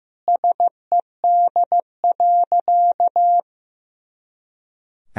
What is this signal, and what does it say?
Morse code, heard as a keyed tone, about 15 wpm, 700 Hz. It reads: SED.
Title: Sentences from Top 200 Words and Abbreviations 15wpm